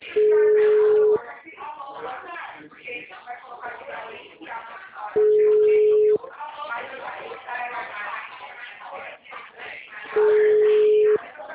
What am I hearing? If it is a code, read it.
Ringback tone (Europe, 425 Hz), with speech.